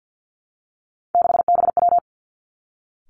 Morse code, heard as a keyed tone, about 50 wpm, 700 Hz. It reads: 6BW
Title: Three Challenging Letter-Number Combinations 50wpm